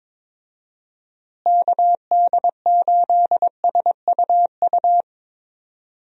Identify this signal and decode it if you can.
Morse code, heard as a keyed tone, about 21 wpm, 700 Hz. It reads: KD8SUU
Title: USA Call Signs - 6 Characters - Repeated 2x - 22wpm